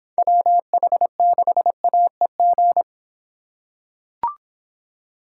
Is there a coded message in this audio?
Yes: Morse code from 0.2 to 2.8 s, heard as a keyed tone, about 24 wpm, 700 Hz. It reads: WH6AEG